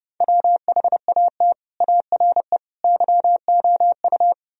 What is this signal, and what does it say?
Morse code, heard as a keyed tone, about 37 wpm, 700 Hz. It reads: WHAT ARE YOU
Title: Speed-Racing - Sets of 5 Words 30wpm